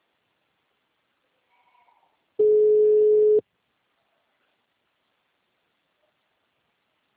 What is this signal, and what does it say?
Ringback tone (Europe, 425 Hz).